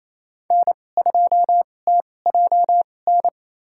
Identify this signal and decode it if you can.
Morse code, heard as a keyed tone, about 27 wpm, spaced as for 20 wpm, 700 Hz. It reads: N2TJN
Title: USA Call Signs - 5 Characters - 2x Character Spacing - 28wpm